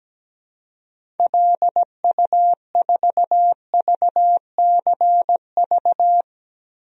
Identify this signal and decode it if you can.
Morse code, heard as a keyed tone, about 17 wpm, 700 Hz. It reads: LU4VCV
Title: DXCC-100 Call Signs - 6 Characters - 17wpm